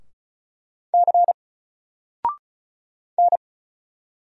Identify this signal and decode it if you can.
Morse code, heard as a keyed tone, about 36 wpm, 700 Hz. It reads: C N
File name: Single-Letter-Number-35wpm